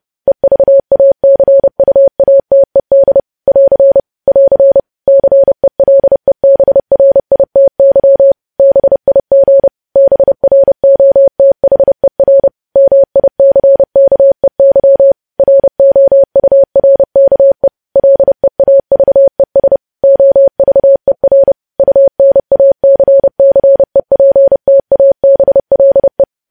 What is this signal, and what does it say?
Morse code, heard as a keyed tone, about 30 wpm, 550 Hz. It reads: EVACUATED <AR> <AR> CELEBRITY BIG BROTHER MICKEY ROURKE LEAVES OVER UNACCEPTABLE
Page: news headlines in morse code at 30 wpm